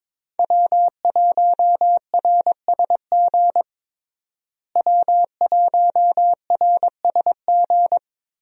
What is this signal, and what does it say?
Morse code, heard as a keyed tone, about 22 wpm, 700 Hz. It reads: W1RSG W1RSG
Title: USA Call Signs - 5 Characters - Repeated 3x - 22wpm